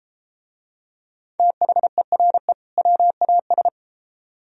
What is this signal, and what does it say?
Morse code, heard as a keyed tone, about 30 wpm, 700 Hz. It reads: THERE WAS